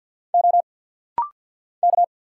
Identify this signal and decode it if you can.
Morse code, heard as a keyed tone, about 50 wpm, 700 Hz. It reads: O K
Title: Single Letters - Rapid-Fire 50wpm